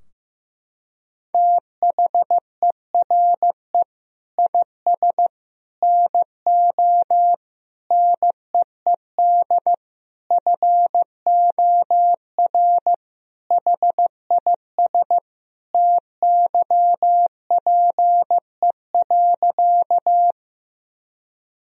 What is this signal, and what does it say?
Morse code, heard as a keyed tone, about 15 wpm, 700 Hz. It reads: THERE IS NO NEED FOR HIS TYPE.
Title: Sentences from Top 1000 Words - Review Part 3 - 15wpm